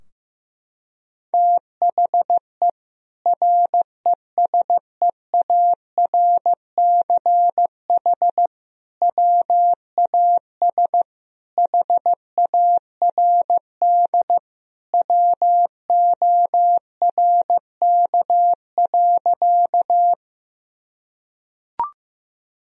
Morse code, heard as a keyed tone, about 15 wpm, 700 Hz. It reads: THE RESEARCH WAS HARD WORK.
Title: Sentences from Top 1000 Words - Review Part 4 - 15wpm